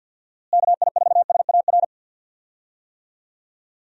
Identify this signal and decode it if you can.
Morse code, heard as a keyed tone, about 50 wpm, 700 Hz. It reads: KI4SAR
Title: USA Call Signs - 6 Characters - Repeated 3x - 50wpm